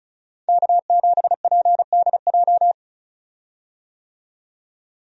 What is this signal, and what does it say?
Morse code, heard as a keyed tone, about 36 wpm, 700 Hz. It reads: K7PDJ
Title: USA Call Signs - 5 Characters - 35wpm